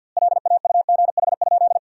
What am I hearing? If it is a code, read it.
Morse code, heard as a keyed tone, about 52 wpm, 700 Hz. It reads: LAUGH?
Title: 2x Word-Spacing - Sentences from Top 500 Words - 50wpm